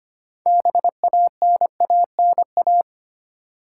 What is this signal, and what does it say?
Morse code, heard as a keyed tone, about 25 wpm, 690 Hz. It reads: BANANA
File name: practice / 25wpm / Easy Words with Hints - 25wpm